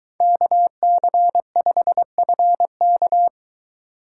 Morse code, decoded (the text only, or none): KC5FK